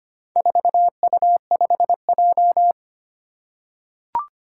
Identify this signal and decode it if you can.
Morse code, heard as a keyed tone, about 25 wpm, 700 Hz. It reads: 4U5J
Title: Four Challenging Letter-Number Combinations 25wpm